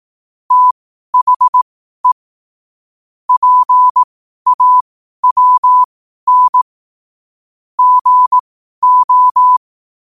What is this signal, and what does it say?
Morse code, heard as a keyed tone, about 18 wpm, spaced as for 12 wpm, 1000 Hz. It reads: THE PAWN GO
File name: qotdfiles / 12wpm